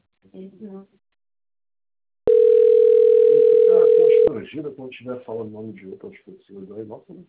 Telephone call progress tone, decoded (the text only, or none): ringback tone